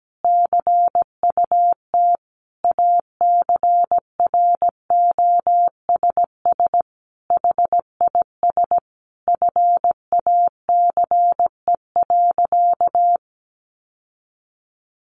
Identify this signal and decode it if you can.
Morse code, heard as a keyed tone, about 17 wpm, 700 Hz. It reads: CUT ACROSS HIS FACE.
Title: Sentences from Top 700 Words 17wpm